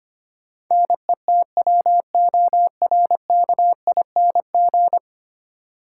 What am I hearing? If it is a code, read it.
Morse code, heard as a keyed tone, about 25 wpm, 700 Hz. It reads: NETWORKING
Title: Practice QSO - Exchange 3 - More Info - 25wpm